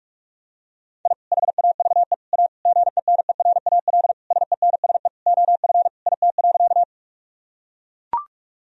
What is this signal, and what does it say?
Morse code, heard as a keyed tone, about 42 wpm, 700 Hz. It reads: I HAVE A GENERAL SENSE OF IT.
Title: Sentences from Top 1000 Words - Review Part 4 - 45wpm